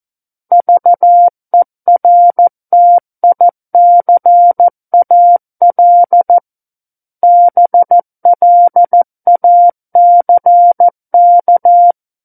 Morse code, heard as a keyed tone, about 14 wpm, 700 Hz. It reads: VERTICAL BLACK
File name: Book One / 05-Chapter 5 - 14 WPM